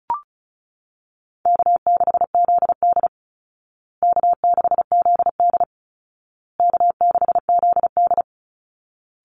Morse code, heard as a keyed tone, about 34 wpm, 700 Hz. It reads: K6ZD K6ZD K6ZD
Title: Call Signs Repeated 2x - 35wpm